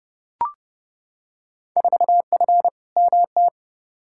Morse code, heard as a keyed tone, about 31 wpm, 700 Hz. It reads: 4F MT
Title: Field Day Exchanges Encore - 30wpm